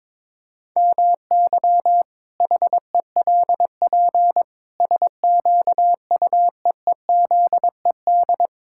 Morse code, heard as a keyed tone, about 22 wpm, 700 Hz. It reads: MY HELP SQUEEZED
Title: Pangram Challenge v2 - 22wpm